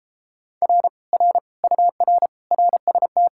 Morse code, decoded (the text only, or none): R R UR RST